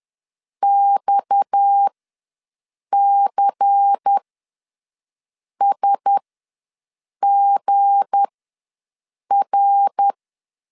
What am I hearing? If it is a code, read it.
Morse code, heard as a keyed tone, about 10 wpm, spaced as for 6 wpm, 800 Hz. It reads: XCSGR
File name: LEZIONI / 8 YJ45 / 8 YJ ws 10wpm 4043